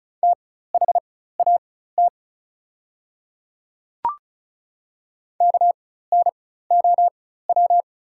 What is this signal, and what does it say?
Morse code, heard as a keyed tone, about 36 wpm, spaced as for 16 wpm, 700 Hz. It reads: THAT KNOW
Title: 4x Character Spacing - Top 100 Words - 35wpm